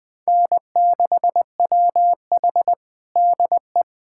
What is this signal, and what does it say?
Morse code, heard as a keyed tone, about 20 wpm, 700 Hz. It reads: N6WH DE
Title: USA Callsign DE Callsign - 4 Characters - 20wpm